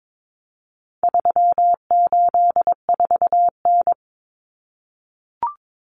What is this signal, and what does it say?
Morse code, heard as a keyed tone, about 22 wpm, 700 Hz. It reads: 384N
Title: Common Cut Numbers - 4-Digit 22wpm